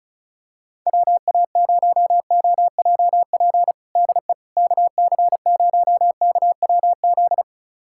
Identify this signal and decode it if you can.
Morse code, heard as a keyed tone, about 34 wpm, 700 Hz. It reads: WA0OJP DE KC0KWZ
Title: USA Callsign DE Callsign - 6 Characters - 35wpm